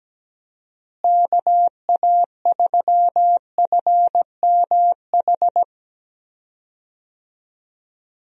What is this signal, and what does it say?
Morse code, heard as a keyed tone, about 17 wpm, 700 Hz. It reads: KA3FMH
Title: USA Call Signs - 6 Characters - Repeated 2x - 17wpm